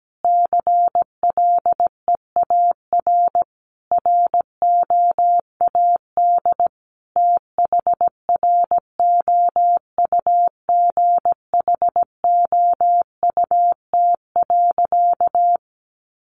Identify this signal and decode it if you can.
Morse code, heard as a keyed tone, about 17 wpm, 700 Hz. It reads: CLEAR ROAD THROUGHOUT.
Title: Sentences from Top 800 Words 17wpm